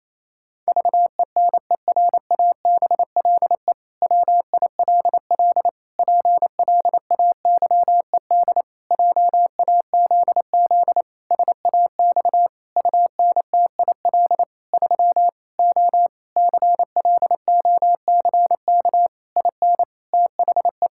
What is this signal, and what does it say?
Morse code, heard as a keyed tone, about 28 wpm, 700 Hz. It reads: VENERABLE WILL PLAYED JAZZ SAX UNTIL 3 O CLOCK IN THE
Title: Pangram Challenge v2 - 28wpm